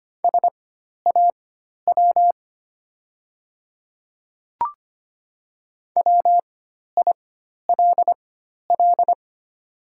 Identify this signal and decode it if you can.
Morse code, heard as a keyed tone, about 25 wpm, spaced as for 12 wpm, 700 Hz. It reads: SAW WILL
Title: 4x Character Spacing - Top 500 Words - 25wpm